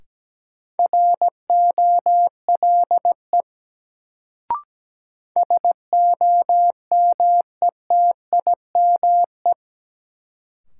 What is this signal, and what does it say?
Morse code, heard as a keyed tone, about 17 wpm, 700 Hz. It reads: ROLE SOMETIME
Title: Sentences from Top 1000 Words - Review Part 6 - New Word Drill - 17wpm